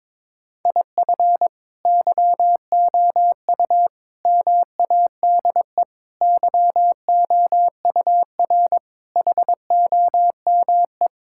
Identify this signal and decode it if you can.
Morse code, heard as a keyed tone, about 22 wpm, 700 Hz. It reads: IF YOU MADE YOUR HOME
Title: Sentences from Top 1600 Words 22wpm